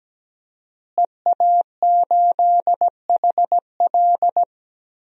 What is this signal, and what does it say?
Morse code, heard as a keyed tone, about 17 wpm, 700 Hz. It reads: EA8HL
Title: DXCC-100 Call Signs - 5 Characters - Repeated 2x - 17wpm